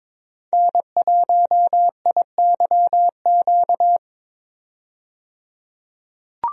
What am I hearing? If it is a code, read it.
Morse code, heard as a keyed tone, about 22 wpm, 700 Hz. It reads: N1IYQ
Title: USA Call Signs - 5 Characters - 22wpm